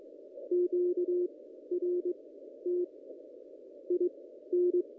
Morse code, heard as a keyed tone, about 22 wpm, spaced as for 12 wpm, 360 Hz. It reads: QRT IN
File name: contests / ss-cw-k5tr-op-n5rz -1700